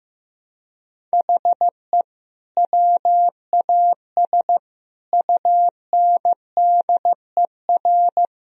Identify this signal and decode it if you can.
Morse code, heard as a keyed tone, about 16 wpm, 700 Hz. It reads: HE WAS UNDER
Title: Sentences from Top 1000 Words - Review Part 1 - 15wpm